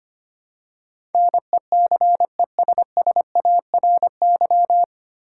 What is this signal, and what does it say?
Morse code, heard as a keyed tone, about 25 wpm, 700 Hz. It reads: NECESSARY
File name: Nine Letter Words - 25wpm